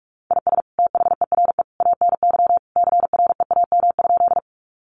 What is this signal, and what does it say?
Morse code, heard as a keyed tone, about 44 wpm, 700 Hz. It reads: IS THERE ANY CREAM?
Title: Sentences from Top 1500 Words 45wpm